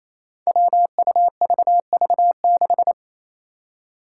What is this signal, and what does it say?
Morse code, heard as a keyed tone, about 28 wpm, 700 Hz. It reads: WUVV6
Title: Five Challenging Letter-Number Combinations 28wpm